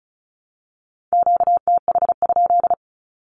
Morse code, heard as a keyed tone, about 36 wpm, 700 Hz. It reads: QTH?
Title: Q-Codes with Spoken Meaning - 35wpm